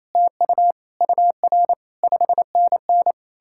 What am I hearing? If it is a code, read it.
Morse code, heard as a keyed tone, about 36 wpm, 700 Hz. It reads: TU UR 5NN